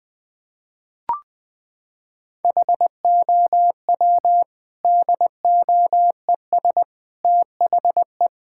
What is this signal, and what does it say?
Morse code, heard as a keyed tone, about 20 wpm, 700 Hz. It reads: HOW DOES THE
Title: Sentences from Top 1000 Words - Review Part 5 - 20wpm